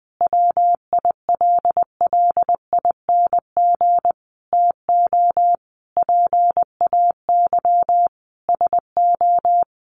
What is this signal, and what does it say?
Morse code, heard as a keyed tone, about 19 wpm, 700 Hz. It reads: WILLING TO PAY SO
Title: Sentences from Top 1400 Words 20wpm